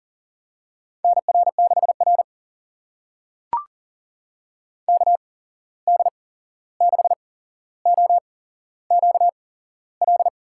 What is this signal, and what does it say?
Morse code, heard as a keyed tone, about 40 wpm, spaced as for 15 wpm, 700 Hz. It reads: NR6R KD6OQL
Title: USA Call Signs - 8x Character Spacing - 40wpm